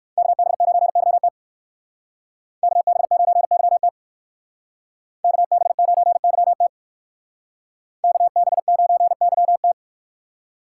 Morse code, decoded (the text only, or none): KB9YT KB9YT KB9YT KB9YT